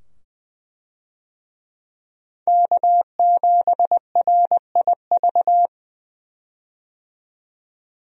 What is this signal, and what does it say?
Morse code, heard as a keyed tone, about 20 wpm, 700 Hz. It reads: K7RIV
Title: USA Call Signs - 5 Characters - 20wpm